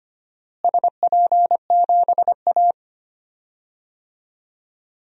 Morse code, heard as a keyed tone, about 25 wpm, 700 Hz. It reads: SP7A
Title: DXCC-100 Call Signs - 4 Characters - Repeated 2x - 25wpm